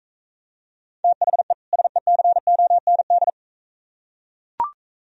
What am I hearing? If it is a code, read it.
Morse code, heard as a keyed tone, about 43 wpm, 690 Hz. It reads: THE SECOND